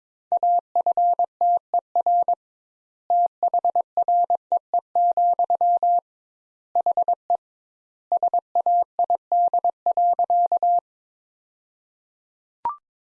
Morse code, heard as a keyed tone, about 22 wpm, 700 Hz. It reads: AFTER THREE, HE SAID.
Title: Sentences Top 200 Words - 22wpm